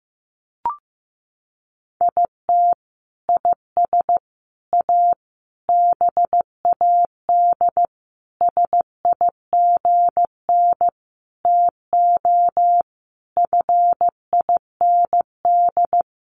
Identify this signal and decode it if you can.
Morse code, heard as a keyed tone, about 14 wpm, 700 Hz. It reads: IT IS A BAD SIGN TO FIND
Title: Sentences from Top 1000 Words 15wpm